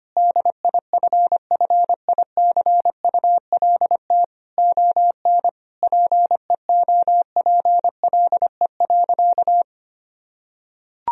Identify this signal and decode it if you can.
Morse code, heard as a keyed tone, about 25 wpm, 700 Hz. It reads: DIFFICULT ON PEOPLE.
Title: Sentences from Top 1000 Words - Review Part 3 - 25wpm